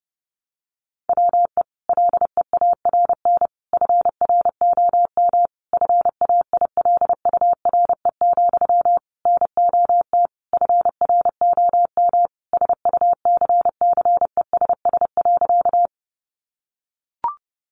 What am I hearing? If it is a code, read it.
Morse code, heard as a keyed tone, about 30 wpm, 700 Hz. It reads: WE LEARN FROM FAILURE, NOT FROM SUCCESS.
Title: Sentences from Top 900 Words 30wpm